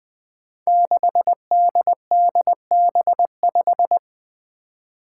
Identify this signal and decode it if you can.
Morse code, heard as a keyed tone, about 20 wpm, 700 Hz. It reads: 6DDB5
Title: Five Challenging Letter-Number Combinations 20wpm